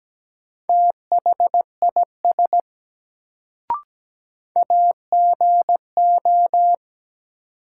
Morse code, heard as a keyed tone, about 17 wpm, 700 Hz. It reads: THIS AGO
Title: Sets of 2 Words - New Word Drill - 17wpm